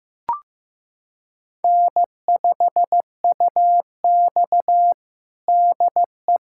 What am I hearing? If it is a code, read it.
Morse code, heard as a keyed tone, about 15 wpm, 700 Hz. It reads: N5UX DE
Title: USA Callsign DE Callsign - 4 Characters - 15wpm